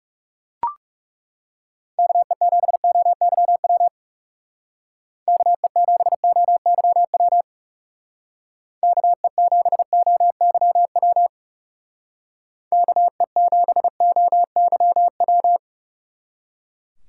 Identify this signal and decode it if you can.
Morse code, heard as a keyed tone, about 37 wpm, 700 Hz. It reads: KE7OYW KE7OYW KE7OYW KE7OYW
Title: Speed Racing - USA 2x3 Call Signs 30wpm